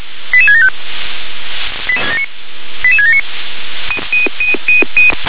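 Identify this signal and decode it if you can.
Five-tone selective call (ZVEI), heard as quick runs of five tones, tones of about 70 ms. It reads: 88675 88679
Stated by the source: Five-tone ZVEI selective call received by SDR.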